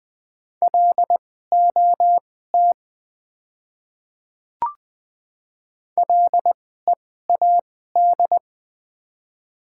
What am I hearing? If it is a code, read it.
Morse code, heard as a keyed tone, about 20 wpm, spaced as for 15 wpm, 690 Hz. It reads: LOT LEAD
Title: 2x Character Spacing - Top 500 Words - 20wpm